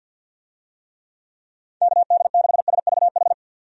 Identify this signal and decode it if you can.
Morse code, heard as a keyed tone, about 50 wpm, 700 Hz. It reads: KD6SVH
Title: USA Call Signs - 6 Characters - 50wpm